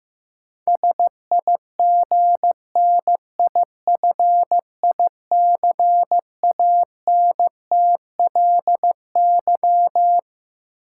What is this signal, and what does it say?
Morse code, heard as a keyed tone, about 15 wpm, 700 Hz. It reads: SIGNIFICANTLY
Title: Sentences from Top 1500 Words - New Word Drill - 15wpm